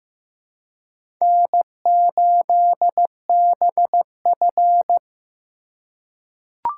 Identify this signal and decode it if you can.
Morse code, heard as a keyed tone, about 15 wpm, 700 Hz. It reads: N8BF
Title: USA Call Signs - 4x Character Spacing - 15wpm